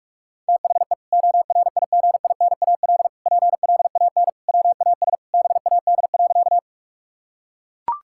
Morse code, heard as a keyed tone, about 41 wpm, 690 Hz. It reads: THE ORIGINAL PLAN WAS BAD.
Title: Sentences from Top 1000 Words - Review Part 6 - 45wpm